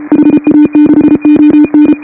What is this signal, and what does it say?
Morse code, heard as a keyed tone, about 29 wpm, 310 Hz. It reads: HA6ON